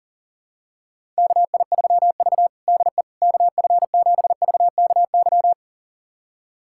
Morse code, heard as a keyed tone, about 40 wpm, 700 Hz. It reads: KI3V DE KF7VKY
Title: USA Callsign DE Callsign 40wpm